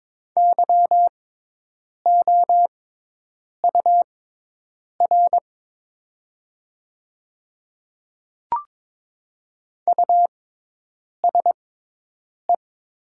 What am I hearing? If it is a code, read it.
Morse code, heard as a keyed tone, about 22 wpm, spaced as for 8 wpm, 700 Hz. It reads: YOUR USE
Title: 6x Character Spacing - Top 100 Words - 22wpm